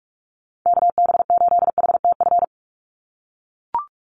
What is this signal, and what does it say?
Morse code, heard as a keyed tone, about 44 wpm, 700 Hz. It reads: KB8HTF
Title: USA Call Signs 45wpm